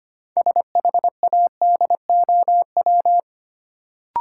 Morse code, heard as a keyed tone, about 25 wpm, 700 Hz. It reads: SHADOW
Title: Sets of 5 Words Encore - New Word Drill - 25wpm